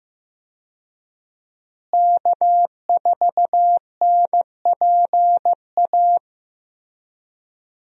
Morse code, heard as a keyed tone, about 15 wpm, 700 Hz. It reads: K4NPA
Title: USA Call Signs - 5 Characters - Repeated 3x - 15wpm